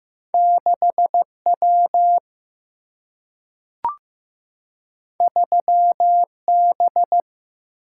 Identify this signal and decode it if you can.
Morse code, heard as a keyed tone, about 15 wpm, 700 Hz. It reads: 6 W 3 B
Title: Single-Letter Single-Number 15wpm